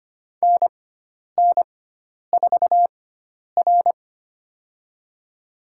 Morse code, heard as a keyed tone, about 25 wpm, spaced as for 10 wpm, 700 Hz. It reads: NN4R
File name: Lesson 16g - Letter c - Call Signs 25wpm